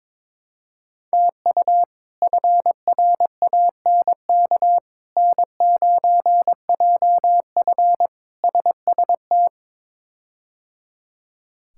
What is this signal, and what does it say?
Morse code, heard as a keyed tone, about 22 wpm, 700 Hz. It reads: TU FRANK N9JF SST
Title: K1USN SST QSO Practice - 22wpm